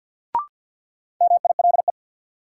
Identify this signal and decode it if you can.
Morse code, heard as a keyed tone, about 53 wpm, 700 Hz. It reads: MILE
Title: Sentences from Top 400 Words - New Word Drill - 50wpm